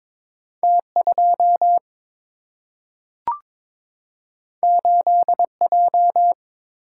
Morse code, heard as a keyed tone, about 22 wpm, 700 Hz. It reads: T 2 8 J
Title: Single-Letter Single-Number 22wpm